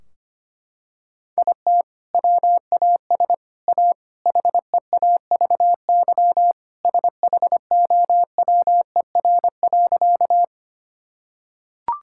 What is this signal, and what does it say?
Morse code, heard as a keyed tone, about 25 wpm, 700 Hz. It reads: IT WAS A HEAVY SHOWER.
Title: Sentences from Top 1600 Words 25wpm